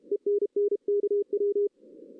Morse code, heard as a keyed tone, about 31 wpm, 410 Hz. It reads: ENN KW
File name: 2016 arrl dx cw / 2016-ARRL-DX-CW-K5TR- - 0-3